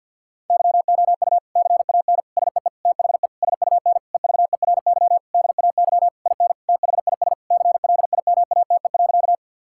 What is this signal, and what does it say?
Morse code, heard as a keyed tone, about 52 wpm, 700 Hz. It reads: YOU CAN SEE THE SUN EVERY DAY IN THIS CLIMATE.
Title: Sentences from Top 1600 Words 50wpm